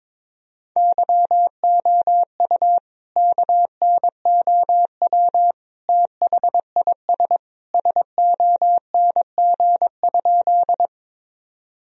Morse code, heard as a keyed tone, about 22 wpm, 700 Hz. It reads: YOU KNOW THIS SONG?
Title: Sentences from Top 1000 Words - Review Part 2 - 22wpm